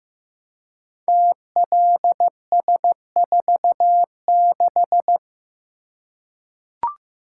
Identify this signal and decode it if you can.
Morse code, heard as a keyed tone, about 15 wpm, 700 Hz. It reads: TLS46
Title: Code Groups - 5 Letter:Numbers - 15wpm